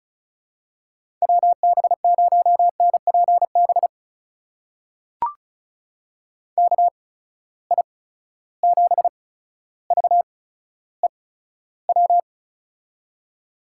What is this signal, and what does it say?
Morse code, heard as a keyed tone, about 36 wpm, spaced as for 15 wpm, 700 Hz. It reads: WB0NPB KI7VEW
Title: USA Call Signs - 6 Characters - 8x Character Spacing - 35wpm